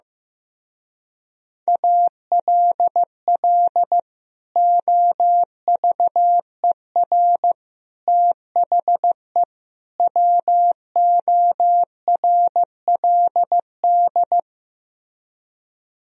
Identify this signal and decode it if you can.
Morse code, heard as a keyed tone, about 16 wpm, 700 Hz. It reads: ALL OVER THE WORLD
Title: Speed-Racing - Sets of 4 Words 15wpm